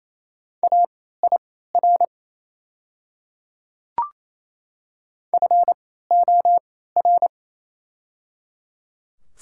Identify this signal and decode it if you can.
Morse code, heard as a keyed tone, about 28 wpm, spaced as for 16 wpm, 700 Hz. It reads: AIR FOR
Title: 3x Character Spacing - Top 200 Words - 28wpm